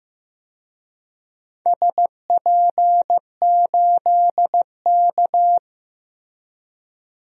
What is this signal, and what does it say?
Morse code, heard as a keyed tone, about 16 wpm, 700 Hz. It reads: SP8K
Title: DXCC-100 Call Signs - 4 Characters - Repeated 2x - 15wpm